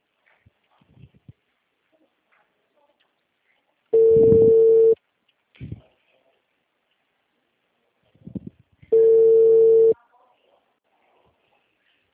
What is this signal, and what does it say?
Ringback tone (China, 450 Hz).
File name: out- -2005- - - .336